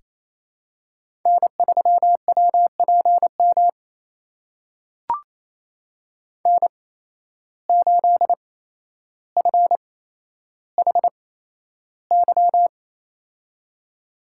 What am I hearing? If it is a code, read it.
Morse code, heard as a keyed tone, about 28 wpm, spaced as for 12 wpm, 700 Hz. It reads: N3WPM N8FHY
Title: USA Call Signs - 5 Characters - 8x Character Spacing - 28wpm